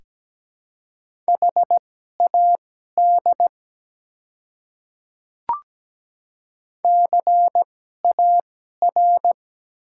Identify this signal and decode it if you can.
Morse code, heard as a keyed tone, about 17 wpm, spaced as for 12 wpm, 700 Hz. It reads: HAD CAR
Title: 2x Character Spacing - Top 300 Words - 17wpm